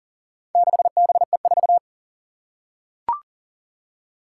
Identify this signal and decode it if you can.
Morse code, heard as a keyed tone, about 40 wpm, 700 Hz. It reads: 6BE4
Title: Four Challenging Letter-Number Combinations 40wpm